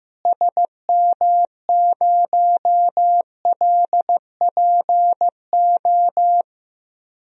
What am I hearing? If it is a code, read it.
Morse code, heard as a keyed tone, about 15 wpm, 700 Hz. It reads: SM0LPO
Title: DXCC-100 Call Signs - 6 Characters - Repeated 2x - 15wpm